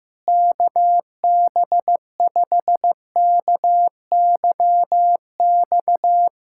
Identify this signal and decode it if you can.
Morse code, heard as a keyed tone, about 15 wpm, 700 Hz. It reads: KB5KYX
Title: K1USN SST QSO Practice - 15wpm